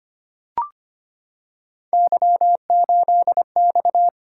Morse code, heard as a keyed tone, about 25 wpm, 700 Hz. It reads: Y8X
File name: Three Letter-Number Combinations - 25wpm